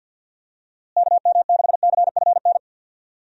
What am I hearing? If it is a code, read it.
Morse code, heard as a keyed tone, about 47 wpm, 700 Hz. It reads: KM6CFN
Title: USA Call Signs - 6 Characters - Repeated 2x - 50wpm